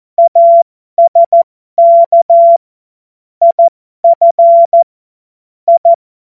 Morse code, heard as a keyed tone, about 14 wpm, spaced as for 12 wpm, 680 Hz. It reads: ASK IF I
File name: Morsekurs / Witze